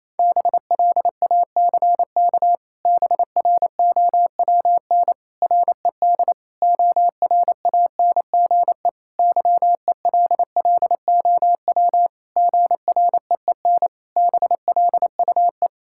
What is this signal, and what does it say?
Morse code, heard as a keyed tone, about 28 wpm, 700 Hz. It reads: BLACK BROWN RED ORANGE YELLOW GREEN BLUE
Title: Pangram Challenge v2 - 28wpm